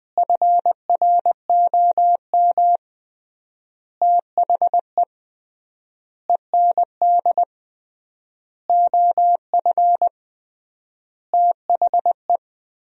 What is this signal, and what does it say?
Morse code, heard as a keyed tone, about 20 wpm, 700 Hz. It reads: FROM THE END OF THE
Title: 3x Word-Spacing - Sentences from Top 200 Words - 20wpm